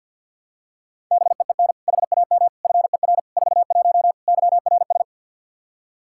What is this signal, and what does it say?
Morse code, heard as a keyed tone, about 52 wpm, 700 Hz. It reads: BEEN HAM FER 41 YRS
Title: Practice QSO - Exchange 3 - More Info - 50wpm